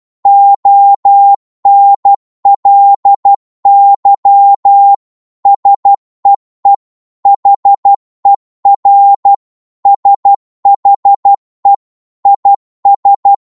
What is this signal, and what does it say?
Morse code, heard as a keyed tone, about 12 wpm, 800 Hz. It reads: ONLY SEE HER SHE IS